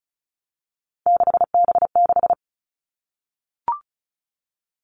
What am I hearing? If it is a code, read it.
Morse code, heard as a keyed tone, about 34 wpm, 700 Hz. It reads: 6B6
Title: All Cut Numbers - 3-Digit 35wpm